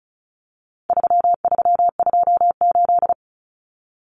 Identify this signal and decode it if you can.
Morse code, heard as a keyed tone, about 34 wpm, 700 Hz. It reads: 3328
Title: Common Cut Numbers - 4-Digit 35wpm